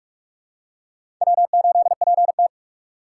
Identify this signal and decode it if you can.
Morse code, heard as a keyed tone, about 46 wpm, 700 Hz. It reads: W8PT
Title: USA Call Signs - 4 Characters - 2x Character Spacing - 45wpm